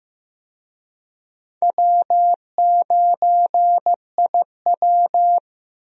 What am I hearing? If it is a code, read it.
Morse code, heard as a keyed tone, about 15 wpm, 700 Hz. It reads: W9IW
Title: USA Call Signs - 4 Characters - Repeated 2x - 15wpm